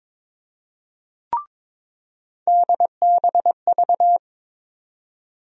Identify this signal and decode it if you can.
Morse code, heard as a keyed tone, about 21 wpm, 700 Hz. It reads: DBV
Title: USA Call Sign Postfixes - 3 Letters 22wpm